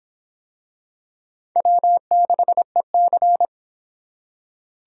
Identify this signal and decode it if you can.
Morse code, heard as a keyed tone, about 25 wpm, 700 Hz. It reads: W6EC